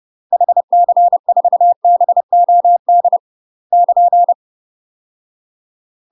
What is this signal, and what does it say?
Morse code, heard as a keyed tone, about 30 wpm, 700 Hz. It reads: HC4BOD <KN>
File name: UK MASTER QSOs TRAINING / 30WPM / UK MASTER QSOs 0034